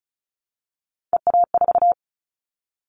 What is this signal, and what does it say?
Morse code, heard as a keyed tone, about 34 wpm, 700 Hz. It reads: EA4
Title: Callsign Prefixes for Top 25 DXCC - Repeated 3x - 35wpm